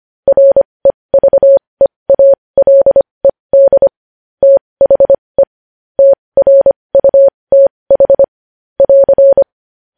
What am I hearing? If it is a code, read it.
Morse code, heard as a keyed tone, about 25 wpm, spaced as for 20 wpm, 550 Hz. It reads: REVEALED THE TRUTH <AR>
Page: News headlines in morse code at 20 wpm